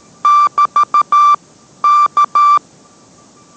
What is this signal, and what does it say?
Morse code, heard as a keyed tone, about 13 wpm, 1180 Hz. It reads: <BT> K